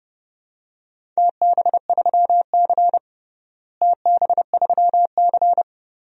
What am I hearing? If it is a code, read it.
Morse code, heard as a keyed tone, about 30 wpm, 700 Hz. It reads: TB3C TB3C
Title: DXCC-100 Call Signs - 4 Characters - Repeated 2x - 30wpm